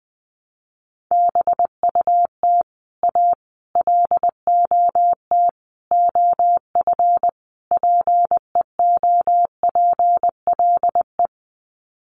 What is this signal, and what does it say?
Morse code, heard as a keyed tone, about 20 wpm, 700 Hz. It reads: BUT A LOT OF PEOPLE
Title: Sets of 5 Words Encore2 20wpm